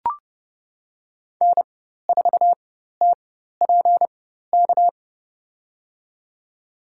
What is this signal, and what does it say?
Morse code, heard as a keyed tone, about 31 wpm, spaced as for 14 wpm, 700 Hz. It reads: N4TPK
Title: USA Call Signs - 4x Character Spacing - 30wpm